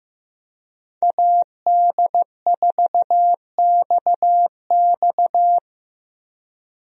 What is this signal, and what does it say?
Morse code, heard as a keyed tone, about 16 wpm, 700 Hz. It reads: AD4XX
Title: USA Call Signs - 5 Characters - Repeated 3x - 15wpm